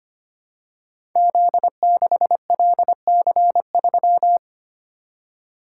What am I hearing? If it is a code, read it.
Morse code, heard as a keyed tone, about 25 wpm, 700 Hz. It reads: Z6LC3
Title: Code Groups - 5 Letter:Numbers - 25wpm